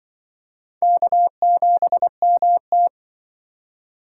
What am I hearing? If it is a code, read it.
Morse code, heard as a keyed tone, about 24 wpm, 700 Hz. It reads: K7MT
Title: Speed Racing - USA 1x2 Call Signs 20wpm